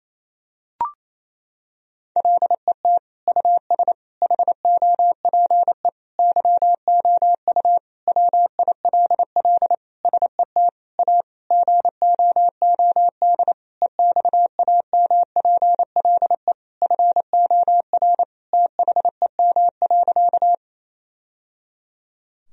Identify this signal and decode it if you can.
Morse code, heard as a keyed tone, about 28 wpm, 700 Hz. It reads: LET US HOPE YOU WILL SET A GOOD EXAMPLE FOR THEM.
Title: Sentences from Top 1000 Words - Review Part 1 - 28wpm